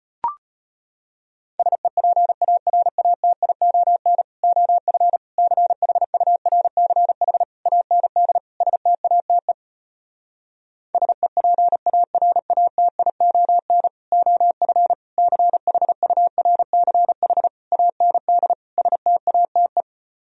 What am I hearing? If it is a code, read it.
Morse code, heard as a keyed tone, about 33 wpm, 700 Hz. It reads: SEPARATION OF CHURCH AND STATE SEPARATION OF CHURCH AND STATE